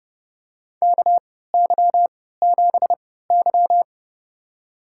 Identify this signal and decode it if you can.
Morse code, heard as a keyed tone, about 30 wpm, spaced as for 17 wpm, 700 Hz. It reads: KY7Y
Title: USA Call Signs - 3x Character Spacing - 30wpm